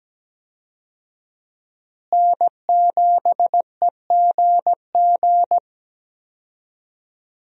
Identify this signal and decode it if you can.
Morse code, heard as a keyed tone, about 17 wpm, 700 Hz. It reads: N7EGG
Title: USA Call Signs - 5 Characters - Repeated 3x - 17wpm